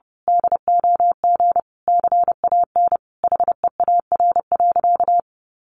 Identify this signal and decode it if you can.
Morse code, heard as a keyed tone, about 30 wpm, 700 Hz. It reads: DOG CAN HEAR.
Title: A Sound Of Thunder 30wpm